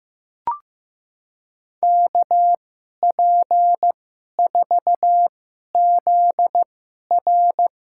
Morse code, heard as a keyed tone, about 15 wpm, spaced as for 11 wpm, 700 Hz. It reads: KP4ZR
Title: USA Call Signs - 5 Characters - 2x Character Spacing - 15wpm